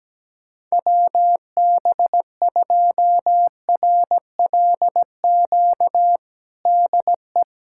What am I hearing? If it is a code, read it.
Morse code, heard as a keyed tone, about 17 wpm, 700 Hz. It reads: WB2RLQ DE
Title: USA Callsign DE Callsign - 6 Characters - 17wpm